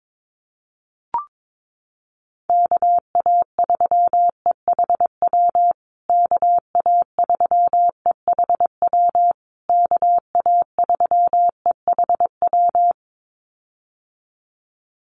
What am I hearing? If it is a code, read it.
Morse code, heard as a keyed tone, about 22 wpm, 700 Hz. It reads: KA3EHW KA3EHW KA3EHW
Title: USA Call Signs Repeated 3x 22wpm